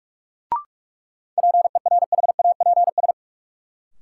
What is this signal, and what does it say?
Morse code, heard as a keyed tone, about 45 wpm, 700 Hz. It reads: PERHAPS
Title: Sentences from Top 500 Words - New Word Drill - 45wpm